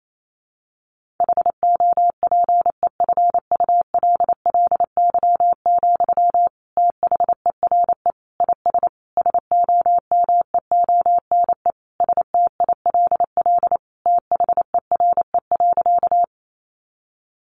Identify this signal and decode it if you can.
Morse code, heard as a keyed tone, about 28 wpm, 700 Hz. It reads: HOPEFULLY, THERE IS SOMEONE STILL THERE.
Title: Sentences from Top 1500 Words 28wpm